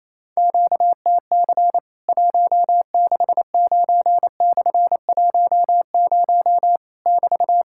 Morse code, heard as a keyed tone, about 28 wpm, 700 Hz. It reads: QTC 169/10 <BT>
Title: Worked All Europe DX Contest - QTC Exchange - 28wpm